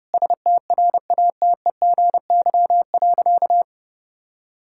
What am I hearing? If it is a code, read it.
Morse code, heard as a keyed tone, about 32 wpm, 700 Hz. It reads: STRATEGY.
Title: Sentences from Top 1000 Words - Review Part 5 - 30wpm